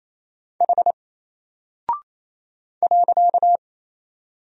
Morse code, heard as a keyed tone, about 28 wpm, 700 Hz. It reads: H .
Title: Single Character (Letter:Number:Punctuation:Prosign) Alt 28wpm